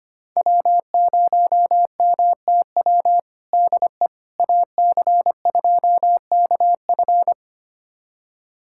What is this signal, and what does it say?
Morse code, heard as a keyed tone, about 25 wpm, 700 Hz. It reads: W0MTW DE AC2KF
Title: USA Callsign DE Callsign - 5 Characters - 25wpm